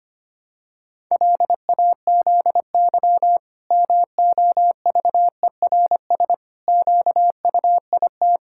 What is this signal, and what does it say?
Morse code, heard as a keyed tone, about 25 wpm, 700 Hz. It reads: LAZY MOVERS QUIT
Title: Pangram Challenge v2 - 25wpm